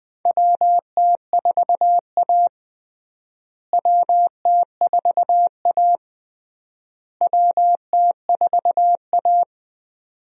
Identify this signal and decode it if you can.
Morse code, heard as a keyed tone, about 20 wpm, 700 Hz. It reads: WT4A WT4A WT4A
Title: USA Call Signs - 4 Characters - Repeated 3x - 20wpm